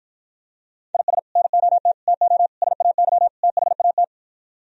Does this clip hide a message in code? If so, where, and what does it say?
Yes: Morse code from 0.9 to 4.0 s, heard as a keyed tone, about 52 wpm, 700 Hz. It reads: IS NOT TO SAY THAT